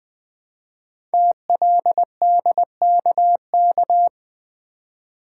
Morse code, heard as a keyed tone, about 20 wpm, 700 Hz. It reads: TLDKK
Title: Code Groups - 5 Letters - 20wpm